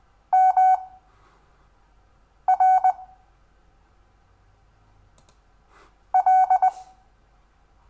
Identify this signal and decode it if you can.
Morse code, heard as a keyed tone, about 20 wpm, spaced as for 5 wpm, 750 Hz. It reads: MR L
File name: cw / Lesson6 WI